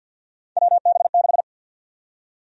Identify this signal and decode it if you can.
Morse code, heard as a keyed tone, about 53 wpm, 700 Hz. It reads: WB6
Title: USA Call Sign Prefixes - 50wpm